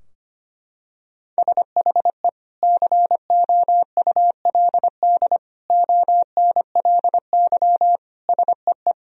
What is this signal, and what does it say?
Morse code, heard as a keyed tone, about 25 wpm, 700 Hz. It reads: SHE COULD ONLY SEE
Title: Sentences from Top 1000 Words - Review Part 2 - 25wpm